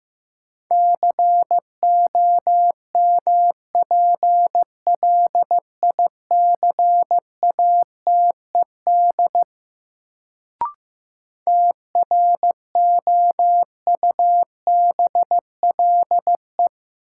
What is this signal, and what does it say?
Morse code, heard as a keyed tone, about 16 wpm, 700 Hz. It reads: COMPLICATED TROUBLE
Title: Sentences from Top 1200 Words - New Word Drill - 15wpm